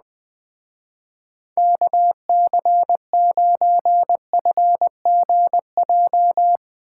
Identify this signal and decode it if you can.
Morse code, heard as a keyed tone, about 20 wpm, 700 Hz. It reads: KC9FGJ
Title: Speed Racing - USA Call Signs 17wpm